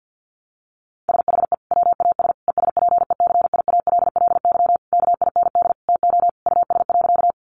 Morse code, heard as a keyed tone, about 50 wpm, 700 Hz. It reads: SHE WAS ESPECIALLY KIND TO US.
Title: Sentences from Top 700 Words 50wpm